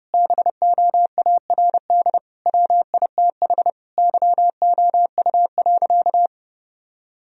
Morse code, heard as a keyed tone, about 32 wpm, 700 Hz. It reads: BOARD WITH YOU.
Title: Sentences from Top 1000 Words - Review Part 4 - 30wpm